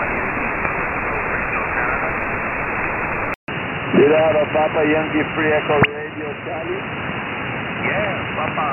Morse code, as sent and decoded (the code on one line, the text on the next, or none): none